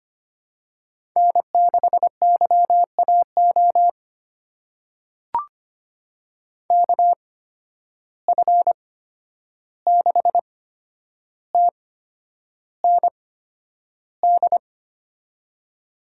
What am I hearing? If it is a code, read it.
Morse code, heard as a keyed tone, about 25 wpm, spaced as for 10 wpm, 700 Hz. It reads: N6YAO KF6TND
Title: USA Call Signs - 8x Character Spacing - 25wpm